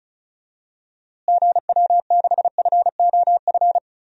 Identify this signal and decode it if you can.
Morse code, heard as a keyed tone, about 34 wpm, 700 Hz. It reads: GW6FOF
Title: DXCC-100 Call Signs - 6 Characters - Repeated 2x - 35wpm